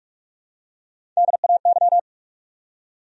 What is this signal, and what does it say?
Morse code, heard as a keyed tone, about 46 wpm, 700 Hz. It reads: DAY
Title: Top 100 Words 45wpm